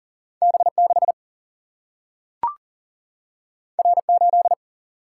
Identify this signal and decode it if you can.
Morse code, heard as a keyed tone, about 40 wpm, 700 Hz. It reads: B6 R8
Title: Single-Letter Single-Number 40wpm